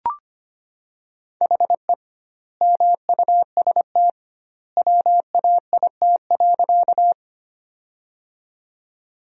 Morse code, heard as a keyed tone, about 25 wpm, 700 Hz. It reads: HE MUST WAIT.
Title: 2x Word-Spacing - Sentences from Top 500 Words - 25wpm